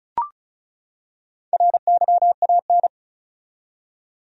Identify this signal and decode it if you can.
Morse code, heard as a keyed tone, about 36 wpm, 700 Hz. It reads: RYAN
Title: USA Common Names - Max 5 Letters - 35wpm